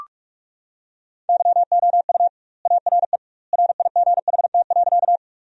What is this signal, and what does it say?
Morse code, heard as a keyed tone, about 46 wpm, 700 Hz. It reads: YOU ARE RIGHT.
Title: 2x Word-Spacing - Sentences from Top 200 Words - 45wpm